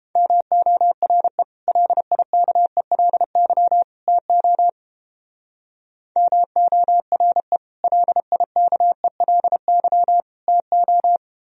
Morse code, heard as a keyed tone, about 31 wpm, 700 Hz. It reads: MORE LIKELY TO MORE LIKELY TO